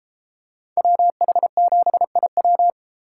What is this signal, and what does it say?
Morse code, heard as a keyed tone, about 31 wpm, 700 Hz. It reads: WH7IW